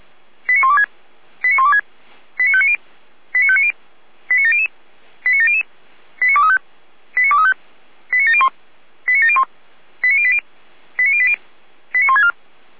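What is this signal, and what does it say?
Five-tone selective call (ZVEI), heard as quick runs of five tones, tones of about 80 ms. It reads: 89126 89126 89590 89590 89700 89700 89235 89235 89701 89701 89080 89080 89164
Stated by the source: ZVEI five-tone call received by SDR.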